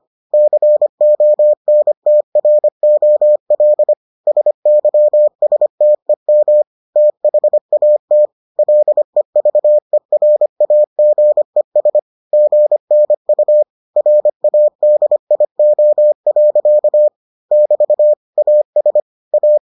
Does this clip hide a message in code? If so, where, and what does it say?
Yes: Morse code from 0.3 to 19.6 s, heard as a keyed tone, about 25 wpm, 600 Hz. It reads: CONTROL SYSTEM THAT LEVERAGES GNU RADIO. <BT> AS A